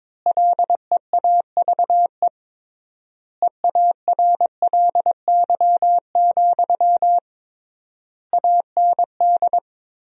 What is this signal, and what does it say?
Morse code, heard as a keyed tone, about 22 wpm, 700 Hz. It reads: LEAVE EARLY, AND
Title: 3x Word-Spacing - Sentences from Top 400 Words - 22wpm